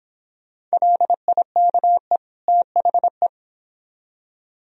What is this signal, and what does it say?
Morse code, heard as a keyed tone, about 25 wpm, 700 Hz. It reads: LIKE THE